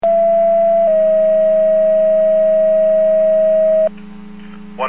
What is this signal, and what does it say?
Two-tone page, heard as a one-second tone, then a three-second tone, about 670 then 640 Hz. It is heard from 0.0 to 3.9 s.